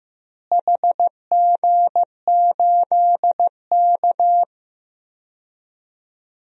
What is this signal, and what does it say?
Morse code, heard as a keyed tone, about 15 wpm, 700 Hz. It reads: HG8K
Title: DXCC-100 Call Signs - 4 Characters - Repeated 3x - 15wpm